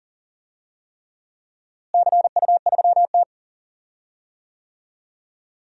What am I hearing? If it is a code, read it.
Morse code, heard as a keyed tone, about 40 wpm, 700 Hz. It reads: CU3T
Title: DXCC-100 Call Signs - 4 Characters - Repeated 3x - 40wpm